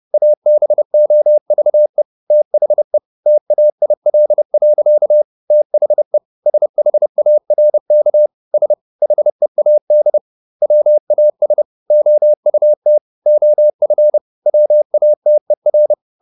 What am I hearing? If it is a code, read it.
Morse code, heard as a keyed tone, about 30 wpm, 600 Hz. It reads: ABOVE THE TAIL. THE SHARK S HEAD WAS OUT OF WATER